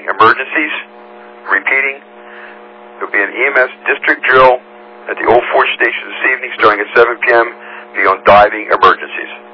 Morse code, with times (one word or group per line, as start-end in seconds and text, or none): none